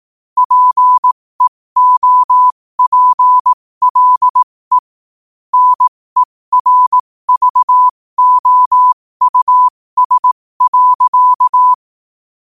Morse code, heard as a keyed tone, about 18 wpm, spaced as for 15 wpm, 1000 Hz. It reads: PEOPLE NERVOUS.
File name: 15wpm